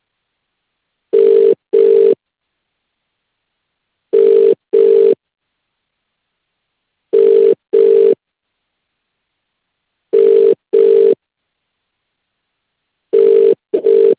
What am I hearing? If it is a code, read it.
Ringback tone (UK), 400 + 450 Hz.